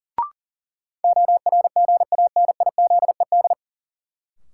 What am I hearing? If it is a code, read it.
Morse code, heard as a keyed tone, about 37 wpm, 700 Hz. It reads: ORGANIZED
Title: Sentences from Top 1300 Words - New Word Drill - 40wpm